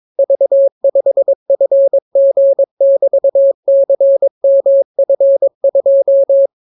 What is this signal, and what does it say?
Morse code, heard as a keyed tone, about 22 wpm, 550 Hz. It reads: V5FG<BT>CMF2